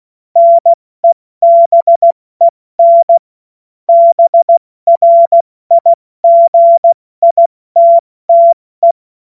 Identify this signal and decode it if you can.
Morse code, heard as a keyed tone, about 16 wpm, 680 Hz. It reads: NEBEN BRIGITTE
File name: Morsekurs / Witze / Witz 04. Januar 2022, 70 F80